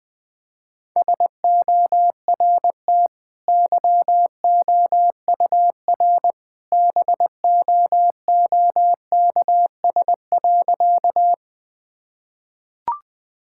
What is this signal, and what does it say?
Morse code, heard as a keyed tone, about 20 wpm, 700 Hz. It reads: SORT YOUR BOOKS.
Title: Easy Sentences with Hints - 20wpm